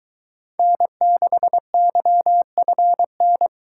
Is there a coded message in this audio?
Yes: Morse code from 0.6 to 3.5 s, heard as a keyed tone, about 23 wpm, 700 Hz. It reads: N6YFN